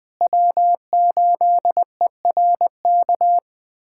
Morse code, heard as a keyed tone, about 21 wpm, 700 Hz. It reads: W8ERK
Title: USA Callsign DE Callsign - 5 Characters - 20wpm